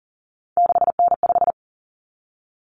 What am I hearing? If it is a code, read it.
Morse code, heard as a keyed tone, about 40 wpm, 700 Hz. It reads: 6N5
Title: Common Cut Numbers - 3-Digit 40wpm